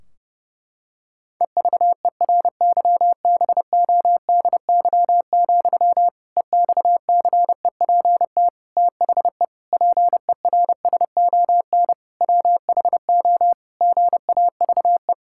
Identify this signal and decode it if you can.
Morse code, heard as a keyed tone, about 32 wpm, 700 Hz. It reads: EVERYBODY, EXCEPT THE PERSON WHO GAVE
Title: Sentences from Top 1600 Words 30wpm